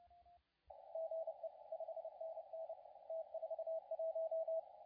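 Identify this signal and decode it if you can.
Morse code, heard as a keyed tone, about 31 wpm, 650 Hz, with radio noise. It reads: GE 5NN T41
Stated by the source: CW contest audio recorded at LA4C during the SAC CW contest.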